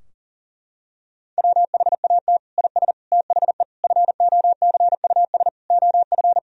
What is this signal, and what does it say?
Morse code, heard as a keyed tone, about 40 wpm, 700 Hz. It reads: WHAT IS THE FOCUS OF
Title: Sentences from Top 1000 Words - Review Part 4 - 40wpm